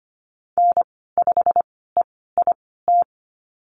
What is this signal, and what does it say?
Morse code, heard as a keyed tone, about 25 wpm, spaced as for 16 wpm, 700 Hz. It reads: N5EIT
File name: Lesson 15g - Number 5 - Call Signs 25wpm